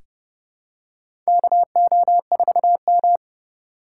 Morse code, heard as a keyed tone, about 31 wpm, 700 Hz. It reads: KO4M
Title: USA Call Signs - 8x Character Spacing - 30wpm